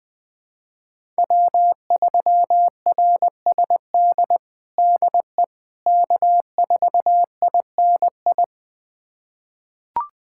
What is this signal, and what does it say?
Morse code, heard as a keyed tone, about 21 wpm, 700 Hz. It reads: W3RSD DE K4INI
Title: USA Callsign DE Callsign - 5 Characters - 20wpm